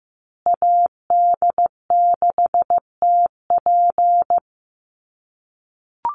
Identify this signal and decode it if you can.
Morse code, heard as a keyed tone, about 15 wpm, 700 Hz. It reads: AD6TP
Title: USA Call Signs Repeated 3x 15wpm